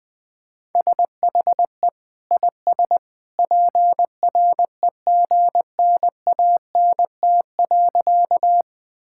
Morse code, heard as a keyed tone, about 20 wpm, 700 Hz. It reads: SHE IS PREGNANT.
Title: Sentences from Top 1000 Words - Review Part 6 - 20wpm